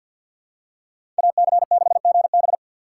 Morse code, heard as a keyed tone, about 50 wpm, 700 Hz. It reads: AC6GB
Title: USA Call Signs - 5 Characters - 2x Character Spacing - 50wpm